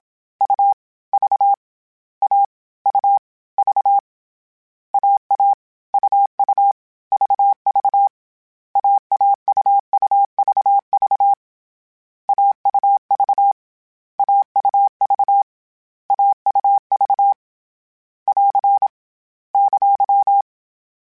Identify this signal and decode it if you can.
Morse code, heard as a keyed tone, about 27 wpm, 800 Hz. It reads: U V A U V AA UU VV AAUUVV AUV AUV AUV <AR> !
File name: Short Runs / Short Runs-mp3s / AUV AUV4 @25